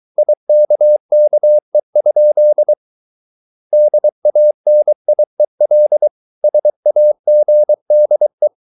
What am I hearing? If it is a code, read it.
Morse code, heard as a keyed tone, about 23 wpm, 600 Hz. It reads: IKKE? DANIEL SAGDE
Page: Ferie 23 wpm